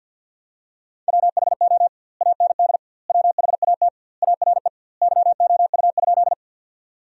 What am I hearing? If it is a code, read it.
Morse code, heard as a keyed tone, about 50 wpm, 700 Hz. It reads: WHO AND WHAT ARE YOU?
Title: 2x Word-Spacing - Sentences from Top 100 Words - 50wpm